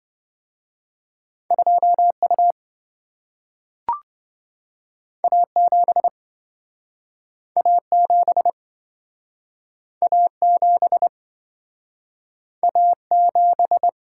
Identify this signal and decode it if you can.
Morse code, heard as a keyed tone, about 26 wpm, 700 Hz. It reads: 2U A7 A7 A7 A7